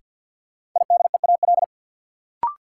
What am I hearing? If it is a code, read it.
Morse code, heard as a keyed tone, about 49 wpm, 700 Hz. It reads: IDEAL
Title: Sentences from Top 1400 Words - New Word Drill - 50wpm